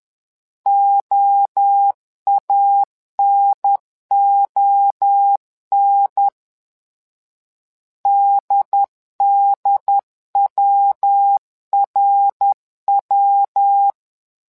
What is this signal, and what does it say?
Morse code, heard as a keyed tone, about 10 wpm, 800 Hz. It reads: OANON DDWRW
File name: LEZIONI / 3 MWRO / 3 MWRO 10wpm 2148